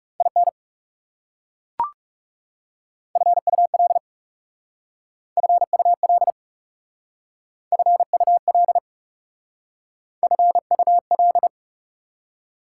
Morse code, heard as a keyed tone, about 40 wpm, 690 Hz. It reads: IN FUL FUL FUL FUL